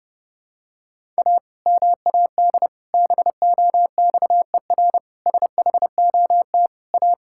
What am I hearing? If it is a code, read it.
Morse code, heard as a keyed tone, about 30 wpm, 700 Hz. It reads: A MAD BOXER SHOT A
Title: Pangram Challenge v2 - 30wpm